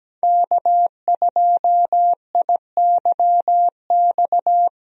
Morse code, heard as a keyed tone, about 17 wpm, 700 Hz. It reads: K2IYX
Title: USA Call Signs - 5 Characters - Repeated 2x - 17wpm